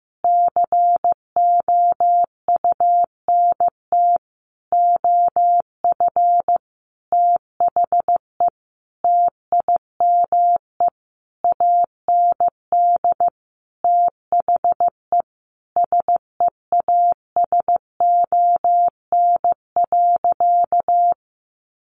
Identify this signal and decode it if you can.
Morse code, heard as a keyed tone, about 15 wpm, 700 Hz. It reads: COUNT OF THE TIME AND THE SEASON.
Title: Sentences from Top 1500 Words 15wpm